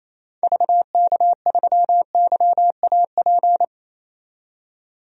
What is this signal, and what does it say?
Morse code, heard as a keyed tone, about 28 wpm, 700 Hz. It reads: VK3YAP
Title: DXCC-100 Call Signs - 6 Characters - 28wpm